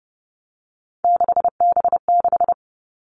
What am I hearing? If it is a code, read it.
Morse code, heard as a keyed tone, about 30 wpm, 700 Hz. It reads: 6B6
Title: All Cut Numbers - 3-Digit 30wpm